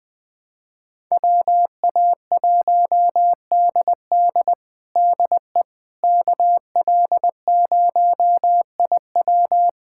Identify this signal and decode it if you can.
Morse code, heard as a keyed tone, about 20 wpm, 700 Hz. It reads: WA1DD DE KL0IW
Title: USA Callsign DE Callsign - 5 Characters - 20wpm